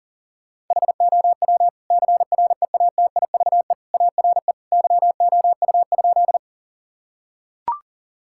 Morse code, heard as a keyed tone, about 40 wpm, 700 Hz. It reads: HOW CREATIVE ARE YOU?
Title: Sentences from Top 1000 Words - Review Part 5 - 40wpm